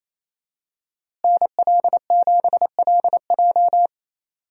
Morse code, heard as a keyed tone, about 28 wpm, 700 Hz. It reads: NL7LJ
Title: USA Call Signs - 5 Characters - Repeated 2x - 28wpm